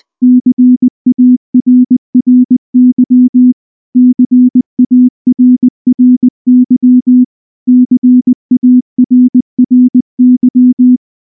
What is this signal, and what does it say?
Morse code, heard as a keyed tone, about 20 wpm, 250 Hz. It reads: CARRY CARRY CARRY
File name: morse / morse-training-20WPM-5